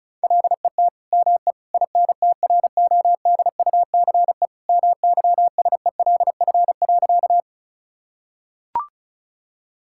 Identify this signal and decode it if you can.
Morse code, heard as a keyed tone, about 36 wpm, 700 Hz. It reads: LET ME INTRODUCE MYSELF.
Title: Sentences from Top 1600 Words 35wpm